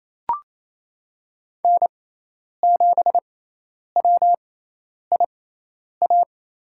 Morse code, heard as a keyed tone, about 27 wpm, spaced as for 10 wpm, 700 Hz. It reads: N7WIA
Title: USA Call Signs - 5 Characters - 6x Character Spacing - 28wpm